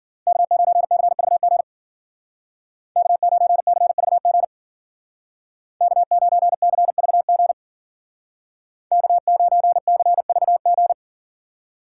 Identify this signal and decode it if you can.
Morse code, heard as a keyed tone, about 53 wpm, 700 Hz. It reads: K9CVG K9CVG K9CVG K9CVG